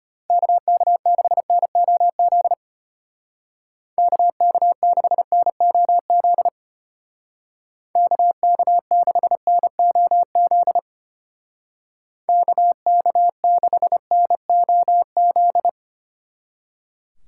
Morse code, heard as a keyed tone, about 31 wpm, 700 Hz. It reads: KK6NOZ KK6NOZ KK6NOZ KK6NOZ